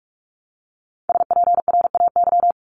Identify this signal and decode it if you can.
Morse code, heard as a keyed tone, about 44 wpm, 700 Hz. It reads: SPRAY
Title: Top 1751-2000 Words 45wpm